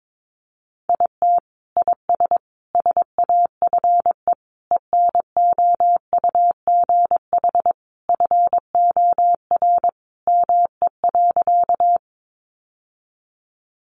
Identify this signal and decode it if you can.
Morse code, heard as a keyed tone, about 22 wpm, 700 Hz. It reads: IT IS SAFE ENOUGH FOR ME.
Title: Sentences from Top 900 Words 22wpm